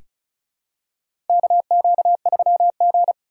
Morse code, heard as a keyed tone, about 34 wpm, 700 Hz. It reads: KQ3G
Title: USA Call Signs - 4x Character Spacing - 35wpm